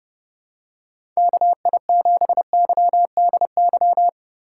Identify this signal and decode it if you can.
Morse code, heard as a keyed tone, about 30 wpm, 700 Hz. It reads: KI7YDY
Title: USA Call Signs - 6 Characters - 3x Character Spacing - 30wpm